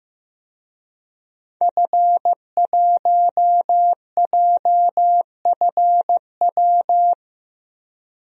Morse code, heard as a keyed tone, about 15 wpm, 700 Hz. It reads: F1JFW
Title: DXCC-100 Call Signs - 5 Characters - 15wpm